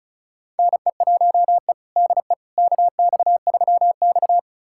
Morse code, heard as a keyed tone, about 36 wpm, 700 Hz. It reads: NE1E DE KX3X
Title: USA Callsign DE Callsign - 4 Characters - 35wpm